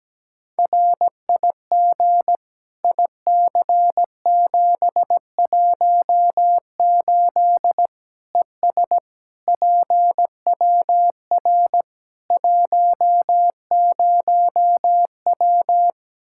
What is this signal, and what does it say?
Morse code, heard as a keyed tone, about 17 wpm, 700 Hz. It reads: RIG IC718 ES PWR 10W
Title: Practice QSO - Exchange 2 - Info - 17wpm